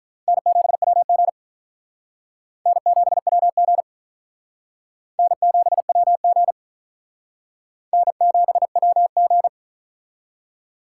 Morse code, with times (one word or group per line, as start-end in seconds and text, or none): 0.3-1.3 N7WG
2.7-3.8 N7WG
5.2-6.5 N7WG
7.9-9.5 N7WG